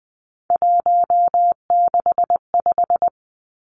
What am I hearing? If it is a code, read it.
Morse code, heard as a keyed tone, about 20 wpm, 700 Hz. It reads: 165
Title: Three Challenging Letter-Number Combinations 20wpm